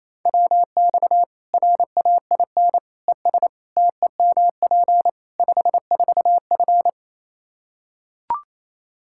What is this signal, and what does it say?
Morse code, heard as a keyed tone, about 28 wpm, 700 Hz. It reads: WX RAIN ES TEMP 54F
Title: Practice QSO - Exchange 2 - Info - 28wpm